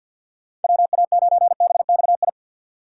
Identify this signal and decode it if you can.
Morse code, heard as a keyed tone, about 50 wpm, 700 Hz. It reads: WA9BXI
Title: USA Call Signs - 6x Character Spacing - 50wpm